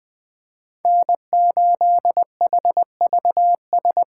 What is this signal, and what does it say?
Morse code, heard as a keyed tone, about 20 wpm, 700 Hz. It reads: N8HVS
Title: USA Call Signs - 5 Characters - 2x Character Spacing - 20wpm